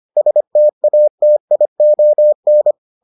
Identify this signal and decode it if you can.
Morse code, heard as a keyed tone, about 25 wpm, 600 Hz. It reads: STATION